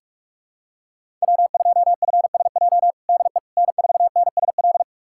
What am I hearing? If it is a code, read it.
Morse code, heard as a keyed tone, about 42 wpm, 700 Hz. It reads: W2FSJ DE N4NSL
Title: USA Callsign DE Callsign - 5 Characters - 45wpm